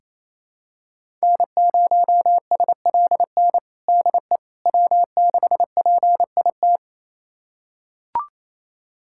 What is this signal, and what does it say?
Morse code, heard as a keyed tone, about 28 wpm, 700 Hz. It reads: N0SLN DE W6PIT
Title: USA Callsign DE Callsign - 5 Characters - 28wpm